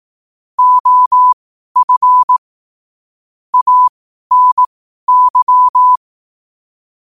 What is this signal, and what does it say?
Morse code, heard as a keyed tone, about 18 wpm, spaced as for 12 wpm, 1000 Hz. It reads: OF ANY
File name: qotdfiles / 12wpm